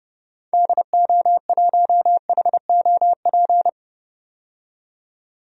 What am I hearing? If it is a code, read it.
Morse code, heard as a keyed tone, about 30 wpm, 700 Hz. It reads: DO1HOP
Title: DXCC-100 Call Signs - 6 Characters - 30wpm